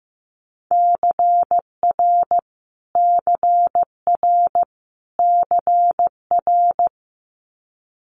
Morse code, heard as a keyed tone, about 15 wpm, 700 Hz. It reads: CR CR CR
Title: Multi-letter Phonemes Repeated 3x 15wpm